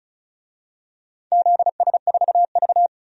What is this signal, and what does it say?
Morse code, heard as a keyed tone, about 34 wpm, 700 Hz. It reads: ZS4V
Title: DXCC-100 Call Signs - 4 Characters - 35wpm